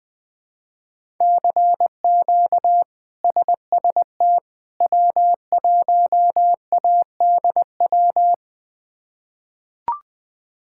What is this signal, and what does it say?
Morse code, heard as a keyed tone, about 20 wpm, 700 Hz. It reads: CQ SST W1ADW
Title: K1USN SST QSO Practice - 20wpm